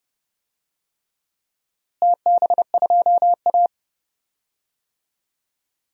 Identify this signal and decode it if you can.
Morse code, heard as a keyed tone, about 30 wpm, 700 Hz. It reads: TB2A
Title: DXCC-100 Call Signs - 4 Characters - Repeated 3x - 30wpm